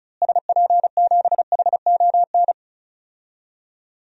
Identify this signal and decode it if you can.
Morse code, heard as a keyed tone, about 36 wpm, 700 Hz. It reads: SP7HON
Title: DXCC-100 Call Signs - 6 Characters - Repeated 2x - 35wpm